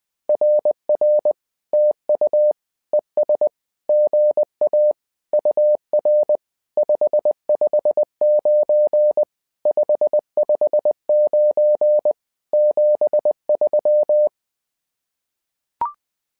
Morse code, heard as a keyed tone, about 20 wpm, 600 Hz. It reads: RR TU ES GA UR 559 559 73
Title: SOTA QSO Practice 20wpm